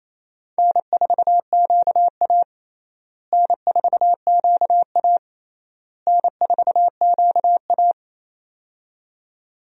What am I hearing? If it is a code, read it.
Morse code, heard as a keyed tone, about 28 wpm, 700 Hz. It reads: N4QA N4QA N4QA
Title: USA Call Signs - 4 Characters - Repeated 3x - 28wpm